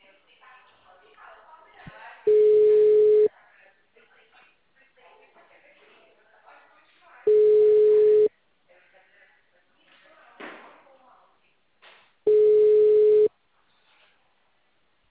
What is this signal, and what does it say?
Ringback tone (Europe, 425 Hz), with speech.
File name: out- -2007- - - .14